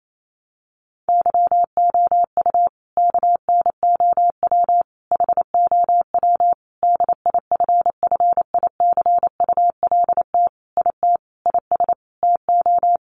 Morse code, heard as a keyed tone, about 28 wpm, 700 Hz. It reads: YOU KNOW HOW DIFFICULT IT IS TO
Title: Sentences from Top 1100 Words 28wpm